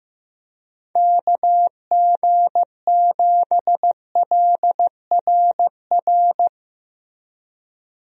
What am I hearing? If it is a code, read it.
Morse code, heard as a keyed tone, about 15 wpm, 700 Hz. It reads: KG7LRR
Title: USA Call Signs - 6 Characters - Repeated 3x - 15wpm